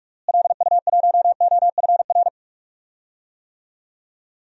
Morse code, heard as a keyed tone, about 43 wpm, 700 Hz. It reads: LU1OFR
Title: DXCC-100 Call Signs - 6 Characters - Repeated 3x - 45wpm